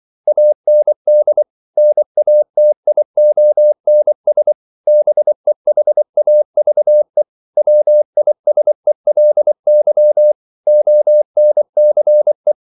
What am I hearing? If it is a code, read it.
Morse code, heard as a keyed tone, about 24 wpm, 600 Hz. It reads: AND NATIONS BEHAVE WISELY ONCE
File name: CW / Practice / EnglishQuotations0424